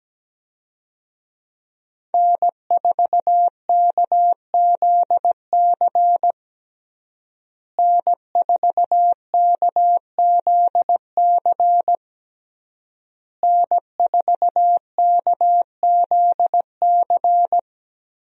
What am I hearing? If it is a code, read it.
Morse code, heard as a keyed tone, about 17 wpm, 700 Hz. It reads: N4KZC N4KZC N4KZC
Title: USA Call Signs - 5 Characters - Repeated 3x - 17wpm